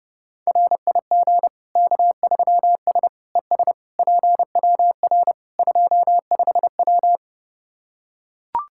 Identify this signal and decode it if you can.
Morse code, heard as a keyed tone, about 32 wpm, 700 Hz. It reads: RIG K3S ES PWR 25W
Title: Practice QSO - Exchange 2 - Info - 30wpm